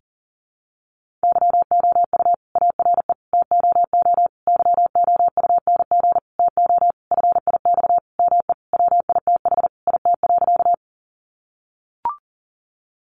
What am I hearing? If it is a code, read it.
Morse code, heard as a keyed tone, about 40 wpm, 700 Hz. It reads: YOU ARE TOO YOUNG TO FIX ME WITH IT.
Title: Sentences from Top 1400 Words 40wpm